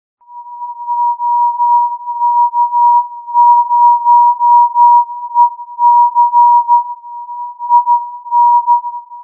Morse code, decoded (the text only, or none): OK0EC IN